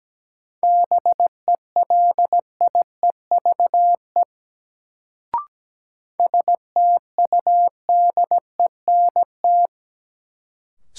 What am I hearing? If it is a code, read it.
Morse code, heard as a keyed tone, about 17 wpm, 700 Hz. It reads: BELIEVE STUDENT
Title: Sentences from Top 1000 Words - New Word Drill - 17wpm